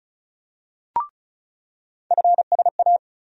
Morse code, heard as a keyed tone, about 36 wpm, 700 Hz. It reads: FSA
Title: USA Call Sign Postfixes - 3 Letters 35wpm